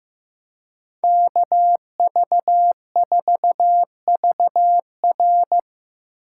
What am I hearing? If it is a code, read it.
Morse code, heard as a keyed tone, about 15 wpm, 700 Hz. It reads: KV4VR
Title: USA Call Signs - 2x Character Spacing - 15wpm